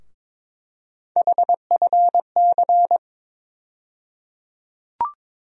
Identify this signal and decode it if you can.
Morse code, heard as a keyed tone, about 22 wpm, 700 Hz. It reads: HFC
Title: USA Call Sign Postfixes - 3 Letters 22wpm